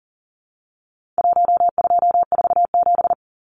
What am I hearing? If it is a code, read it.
Morse code, heard as a keyed tone, about 40 wpm, 700 Hz. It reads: 1247
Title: Numbers - 4-Digit 40wpm